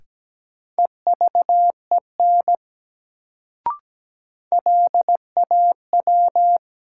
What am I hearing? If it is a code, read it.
Morse code, heard as a keyed tone, about 17 wpm, 700 Hz. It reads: EVEN LAW
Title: Sentences from Top 600 Words - New Word Drill - 17wpm